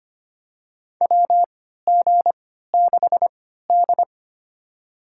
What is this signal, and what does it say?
Morse code, heard as a keyed tone, about 25 wpm, spaced as for 14 wpm, 700 Hz. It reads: WG6D
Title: USA Call Signs - 4 Characters - 3x Character Spacing - 25wpm